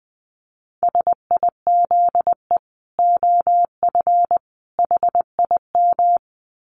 Morse code, heard as a keyed tone, about 20 wpm, 700 Hz. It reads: SIZE OF HIM
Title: Dit Challenge 20wpm